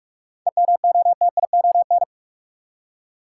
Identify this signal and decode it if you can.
Morse code, heard as a keyed tone, about 46 wpm, 700 Hz. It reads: EMOTION
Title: Seven Letter Words - 45wpm